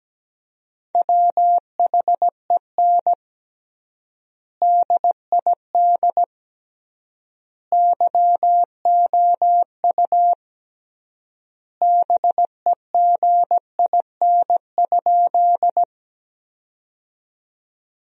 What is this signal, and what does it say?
Morse code, heard as a keyed tone, about 17 wpm, 700 Hz. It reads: WHEN DID YOU BEGIN?
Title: 3x Word-Spacing - Sentences from Top 300 Words - 17wpm